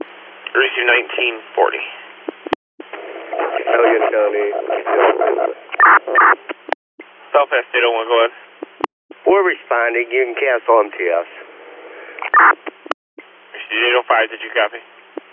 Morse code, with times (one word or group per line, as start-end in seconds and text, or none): none